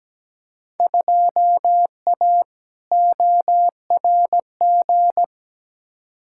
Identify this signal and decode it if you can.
Morse code, heard as a keyed tone, about 17 wpm, 700 Hz. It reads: 2A ORG
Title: Field Day Exchanges Encore - 17wpm